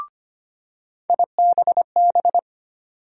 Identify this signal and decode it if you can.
Morse code, heard as a keyed tone, about 25 wpm, 700 Hz. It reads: IBB
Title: USA Call Sign Postfixes - 3 Letters 25wpm